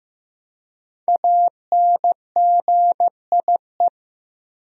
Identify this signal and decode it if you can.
Morse code, heard as a keyed tone, about 15 wpm, 700 Hz. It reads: ANGIE
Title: USA Common Names - Max 5 Letters - 15wpm